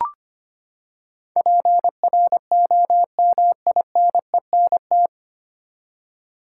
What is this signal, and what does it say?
Morse code, heard as a keyed tone, about 25 wpm, 700 Hz. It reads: PROMINENT
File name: Nine Letter Words - 25wpm